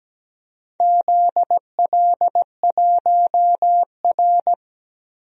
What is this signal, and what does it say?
Morse code, heard as a keyed tone, about 17 wpm, 700 Hz. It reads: ZL1R
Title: DXCC-100 Call Signs - 4 Characters - 17wpm